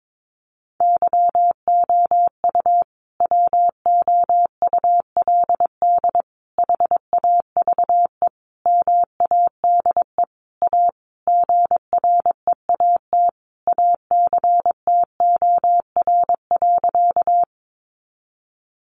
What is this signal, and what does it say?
Morse code, heard as a keyed tone, about 22 wpm, 700 Hz. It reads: YOU WOULD HAVE MADE A GREAT ACTOR.
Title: Sentences from Top 1500 Words 22wpm